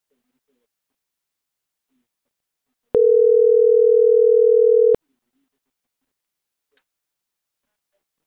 Ringback tone (North America), 440 + 480 Hz.